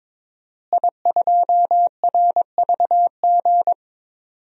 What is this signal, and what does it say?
Morse code, heard as a keyed tone, about 22 wpm, 700 Hz. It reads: I2RVG
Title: DXCC-100 Call Signs - 5 Characters - Repeated 3x - 22wpm